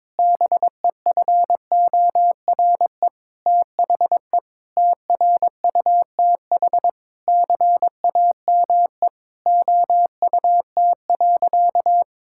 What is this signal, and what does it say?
Morse code, heard as a keyed tone, about 22 wpm, 700 Hz. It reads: BEFORE THE TRUTH CAME OUT.
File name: Sentences with 9-Word Phrases - 22wpm